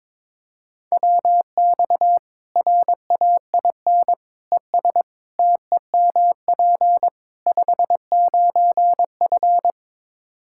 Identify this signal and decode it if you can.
Morse code, heard as a keyed tone, about 21 wpm, 700 Hz. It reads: WX RAIN ES TEMP 59F
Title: Practice QSO - Exchange 2 - Info - 22wpm